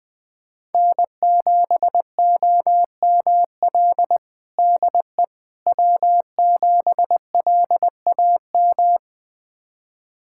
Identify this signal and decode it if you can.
Morse code, heard as a keyed tone, about 20 wpm, 700 Hz. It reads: N7OML DE W7LAM
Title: USA Callsign DE Callsign - 5 Characters - 20wpm